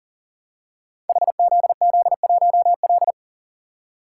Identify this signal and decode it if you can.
Morse code, heard as a keyed tone, about 40 wpm, 700 Hz. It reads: HZZ1L
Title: Code Groups - 5 Letter:Numbers - 40wpm